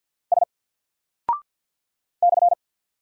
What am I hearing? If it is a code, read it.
Morse code, heard as a keyed tone, about 50 wpm, 700 Hz. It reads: S /
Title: Single Character (Letter:Number:Punctuation:Prosign) 50wpm